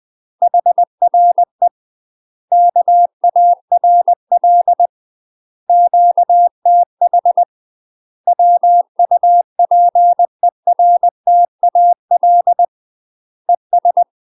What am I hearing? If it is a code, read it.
Morse code, heard as a keyed tone, about 20 wpm, 700 Hz. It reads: HRE KARL QTH WUPERTAL ES
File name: 20WPM